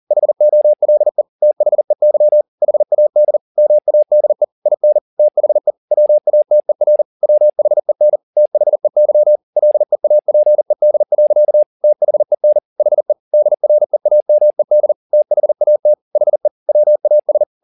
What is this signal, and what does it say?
Morse code, heard as a keyed tone, about 40 wpm, 600 Hz. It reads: HOLE THEY HAD MADE IN THE WATER WHEN THEY LEAPED. THEN HE DREAMED THAT HE WAS
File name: the old man and the sea 2 40wpm